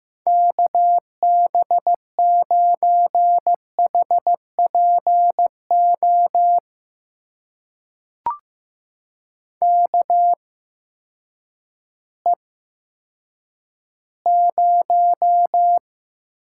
Morse code, heard as a keyed tone, about 15 wpm, spaced as for 9 wpm, 700 Hz. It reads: KB9HPO KE0
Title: USA Call Signs - 6 Characters - 8x Character Spacing - 15wpm